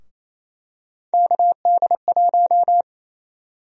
Morse code, heard as a keyed tone, about 28 wpm, 700 Hz. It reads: KD1
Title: USA Call Sign Prefixes - Repeated 2x - 28wpm